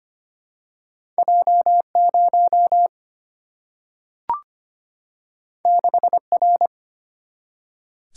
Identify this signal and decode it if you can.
Morse code, heard as a keyed tone, about 25 wpm, 700 Hz. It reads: J0 6R
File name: Two Letter-Number Combinations - 25wpm